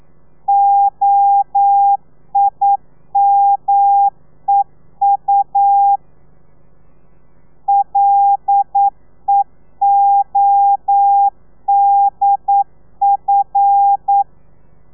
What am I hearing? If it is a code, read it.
Morse code, heard as a keyed tone, about 9 wpm, 800 Hz. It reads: OIMEU LEODF
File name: telegrafia / i2tfj / Lez5 40